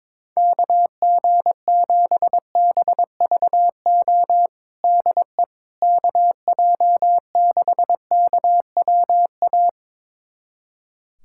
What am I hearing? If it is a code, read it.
Morse code, heard as a keyed tone, about 22 wpm, 700 Hz. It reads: KG7BVO DE KJ6KWA
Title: USA Callsign DE Callsign - 6 Characters - 22wpm